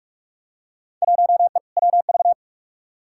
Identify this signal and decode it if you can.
Morse code, heard as a keyed tone, about 46 wpm, 700 Hz. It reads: 1E WV
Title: Field Day Exchanges Encore - 45wpm